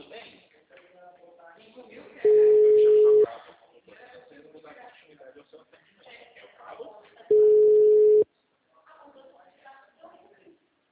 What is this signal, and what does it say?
Ringback tone (Europe, 425 Hz), with speech.